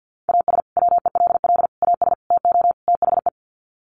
Morse code, heard as a keyed tone, about 50 wpm, 700 Hz. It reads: AS WELL AS TO THE
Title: Sets of 5 Words Encore2 50wpm